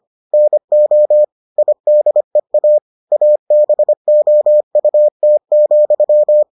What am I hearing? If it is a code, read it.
Morse code, heard as a keyed tone, about 25 wpm, 600 Hz. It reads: NO IDEA ABOUT,